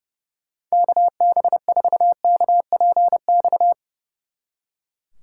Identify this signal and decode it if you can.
Morse code, heard as a keyed tone, about 38 wpm, 700 Hz. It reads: KB4KPX